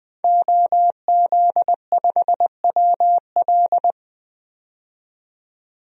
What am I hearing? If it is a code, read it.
Morse code, heard as a keyed tone, about 20 wpm, 700 Hz. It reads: OZ5WL
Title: DXCC-100 Call Signs - 5 Characters - 20wpm